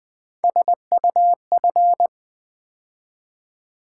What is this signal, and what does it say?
Morse code, heard as a keyed tone, about 20 wpm, 700 Hz. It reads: SUF
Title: Build A Word - 3 Letters At A Time - 9-Letter Words - 20wpm